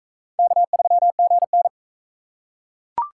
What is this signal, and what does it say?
Morse code, heard as a keyed tone, about 38 wpm, 700 Hz. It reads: K3GN